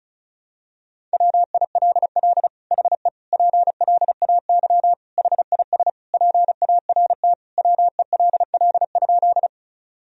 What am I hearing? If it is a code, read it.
Morse code, heard as a keyed tone, about 34 wpm, 700 Hz. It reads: WILL HE PLAY HIS PART WELL?
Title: Sentences from Top 1000 Words - Review Part 1 - 35wpm